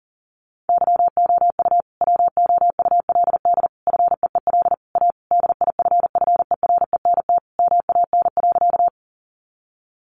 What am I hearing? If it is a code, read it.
Morse code, heard as a keyed tone, about 40 wpm, 700 Hz. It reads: YOU WOULD FEEL A DIFFERENT MAN.
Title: Sentences from Top 600 Words 40wpm